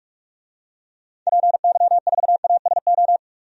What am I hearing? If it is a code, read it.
Morse code, heard as a keyed tone, about 42 wpm, 700 Hz. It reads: PY4ASO
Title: DXCC-100 Call Signs - 6 Characters - 45wpm